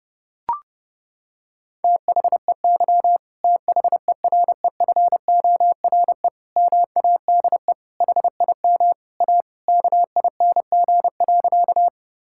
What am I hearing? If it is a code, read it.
Morse code, heard as a keyed tone, about 30 wpm, 700 Hz. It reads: THEY THEREFORE MADE HIM A KING.
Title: Sentences from Top 1000 Words - Review Part 5 - 30wpm